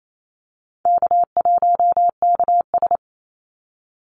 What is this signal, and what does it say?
Morse code, heard as a keyed tone, about 28 wpm, 700 Hz. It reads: K1KS
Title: Easy Call Signs - 28wpm